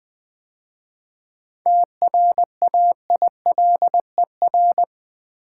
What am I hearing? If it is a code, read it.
Morse code, heard as a keyed tone, about 20 wpm, 700 Hz. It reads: TRAILER
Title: 7 Letter Words - First 3 Letters Given - 20wpm